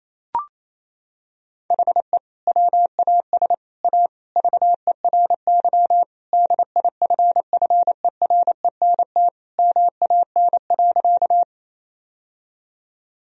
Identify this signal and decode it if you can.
Morse code, heard as a keyed tone, about 27 wpm, 700 Hz. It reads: HE WAS A VERY DIFFERENT MAN.
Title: Sentences from Top 1000 Words - Review Part 3 - 28wpm